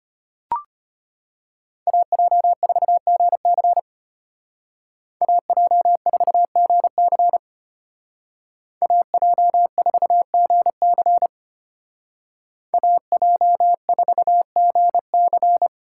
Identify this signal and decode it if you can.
Morse code, heard as a keyed tone, about 30 wpm, 700 Hz. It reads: AJ4GC AJ4GC AJ4GC AJ4GC
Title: Speed Racing - USA 2x2 Call Signs 25wpm